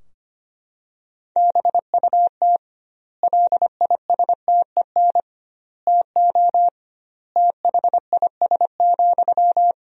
Morse code, heard as a keyed tone, about 25 wpm, 700 Hz. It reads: BUT LISTEN TO THIS,
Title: 2x Word-Spacing - Sentences from Top 400 Words - 25wpm